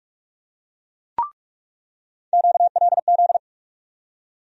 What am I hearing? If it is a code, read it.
Morse code, heard as a keyed tone, about 44 wpm, 700 Hz. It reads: QLZ
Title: USA Call Sign Postfixes - 3 Letters 45wpm